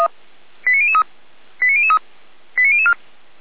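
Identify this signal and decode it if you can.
Five-tone selective call (ZVEI), heard as quick runs of five tones, tones of about 70 ms. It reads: 89003 89003 89004
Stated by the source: ZVEI five-tone call received by SDR.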